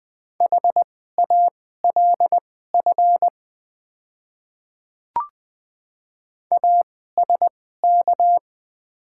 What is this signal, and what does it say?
Morse code, heard as a keyed tone, about 20 wpm, spaced as for 14 wpm, 700 Hz. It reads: HALF ASK
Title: 2x Character Spacing - Top 500 Words - 20wpm